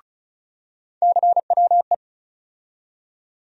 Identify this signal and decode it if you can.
Morse code, heard as a keyed tone, about 34 wpm, 700 Hz. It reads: CWE
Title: USA Call Sign Postfixes - 3 Letters 35wpm